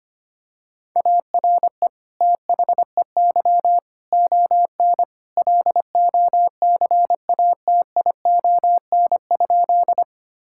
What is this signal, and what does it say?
Morse code, heard as a keyed tone, about 25 wpm, 700 Hz. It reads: ARE THEY ON LOCATION?
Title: Sentences from Top 1000 Words - Review Part 6 - 25wpm